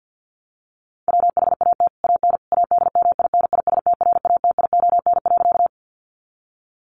Morse code, heard as a keyed tone, about 48 wpm, 700 Hz. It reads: WHAT AN ADMINISTRATION.
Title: Sentences from Top 1300 Words 50wpm